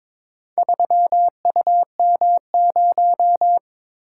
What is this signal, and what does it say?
Morse code, heard as a keyed tone, about 22 wpm, 700 Hz. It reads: 3UM0
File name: practice / 22wpm / Four Letter-Number Combinations - 22wpm